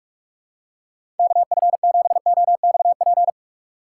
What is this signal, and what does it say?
Morse code, heard as a keyed tone, about 46 wpm, 700 Hz. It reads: KF7OXP
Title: USA Call Signs - 6 Characters - 3x Character Spacing - 45wpm